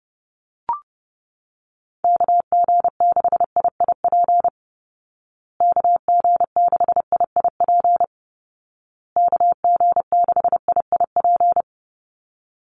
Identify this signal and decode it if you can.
Morse code, heard as a keyed tone, about 30 wpm, 700 Hz. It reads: KG6IIP KG6IIP KG6IIP
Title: Call Signs Repeated 2x - 30wpm